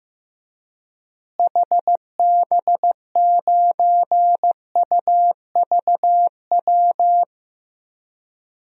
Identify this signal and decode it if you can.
Morse code, heard as a keyed tone, about 15 wpm, 700 Hz. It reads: HB9UVW
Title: DXCC-100 Call Signs - 6 Characters - Repeated 2x - 15wpm